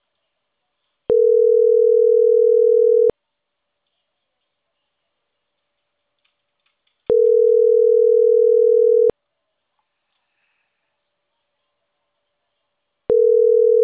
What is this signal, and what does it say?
Ringback tone (North America), 440 + 480 Hz.